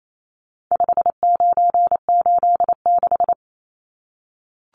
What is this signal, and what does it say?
Morse code, heard as a keyed tone, about 28 wpm, 700 Hz. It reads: 5986
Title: Numbers - 4-Digit 28wpm